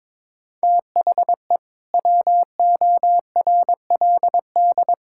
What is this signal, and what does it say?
Morse code, heard as a keyed tone, about 22 wpm, 700 Hz. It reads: THE WORLD
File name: Build A Phrase - 2-Word Phrases - 22wpm